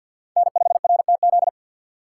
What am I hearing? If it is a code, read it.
Morse code, heard as a keyed tone, about 53 wpm, 700 Hz. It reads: N5RTZ
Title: USA Call Signs - 5 Characters - 8x Character Spacing - 50wpm